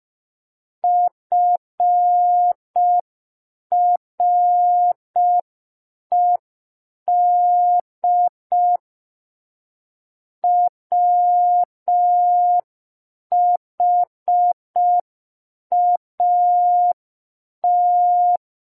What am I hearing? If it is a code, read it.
Morse code, heard as a keyed tone, about 5 wpm, 710 Hz. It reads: FRED WHAT